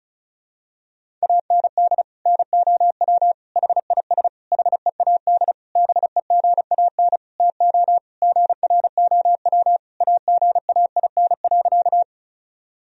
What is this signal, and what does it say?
Morse code, heard as a keyed tone, about 34 wpm, 700 Hz. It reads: AND NOW HIS HEAD BEGAN TO GROW AGAIN.
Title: Sentences from Top 1000 Words - Review Part 1 - 35wpm